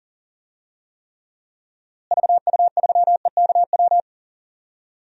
Morse code, heard as a keyed tone, about 37 wpm, 700 Hz. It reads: VU3EKW
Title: DXCC-100 Call Signs - 6 Characters - Repeated 3x - 40wpm